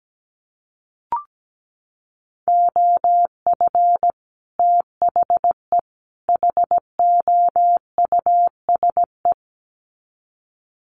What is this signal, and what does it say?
Morse code, heard as a keyed tone, about 17 wpm, 700 Hz. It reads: OF THE HOUSE
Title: Sets of 3 Words 17wpm